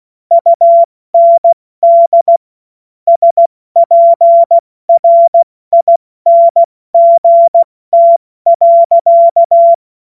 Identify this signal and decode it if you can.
Morse code, heard as a keyed tone, about 16 wpm, 680 Hz. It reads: UND SPRINGT.
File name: Morsekurs / Witze / Witz 16. Februar 2021, 70 F80